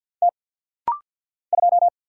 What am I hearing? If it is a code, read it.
Morse code, heard as a keyed tone, about 50 wpm, 700 Hz. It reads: T 2
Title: Single Letter-Number - Rapid-Fire 50wpm